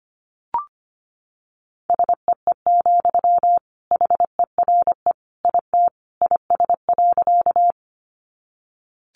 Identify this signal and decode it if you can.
Morse code, heard as a keyed tone, about 25 wpm, 700 Hz. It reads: SEE, HERE IT IS.
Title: Sentences from Top 200 Words 25wpm